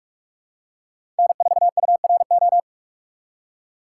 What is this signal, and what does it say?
Morse code, heard as a keyed tone, about 46 wpm, 700 Hz. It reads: N4URO
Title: USA Call Signs - 5 Characters - 3x Character Spacing - 45wpm